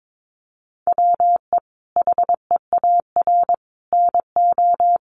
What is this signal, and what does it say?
Morse code, heard as a keyed tone, about 22 wpm, 700 Hz. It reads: WE HEAR NO
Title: Sentences from Top 1400 Words 22wpm